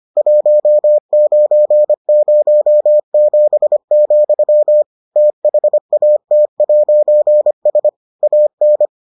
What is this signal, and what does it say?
Morse code, heard as a keyed tone, about 25 wpm, 600 Hz. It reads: 1907, THAT'S AN